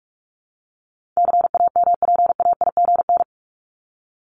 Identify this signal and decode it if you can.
Morse code, heard as a keyed tone, about 44 wpm, 700 Hz. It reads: CAMPAIGN
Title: Top 1001-1250 Words 45wpm